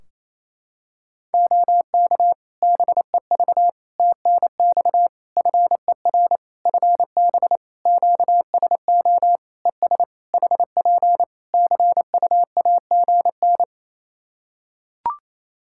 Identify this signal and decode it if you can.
Morse code, heard as a keyed tone, about 28 wpm, 700 Hz. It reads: OK BEV TNX FER FB QSO ES HP CUAGN
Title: Practice QSO - Exchange 4 - Ending - 28wpm